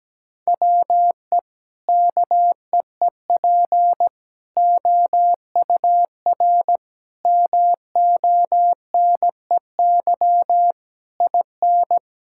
Morse code, heard as a keyed tone, about 17 wpm, 700 Hz. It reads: WE KEEP OUR MONEY IN
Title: Sentences from Top 1000 Words - Review Part 6 - 17wpm